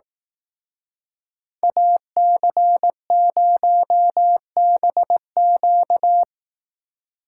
Morse code, heard as a keyed tone, about 18 wpm, 700 Hz. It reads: AC0BQ
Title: Speed Racing - USA Call Signs 15wpm